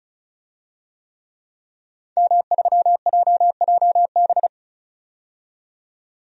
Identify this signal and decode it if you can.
Morse code, heard as a keyed tone, about 36 wpm, 700 Hz. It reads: M3JJB
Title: DXCC-100 Call Signs - 5 Characters - 35wpm